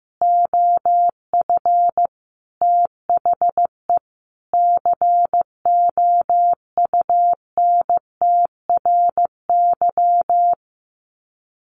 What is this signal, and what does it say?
Morse code, heard as a keyed tone, about 15 wpm, 700 Hz. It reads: OF THE COUNTRY
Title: Sets of 5 Words Encore 15wpm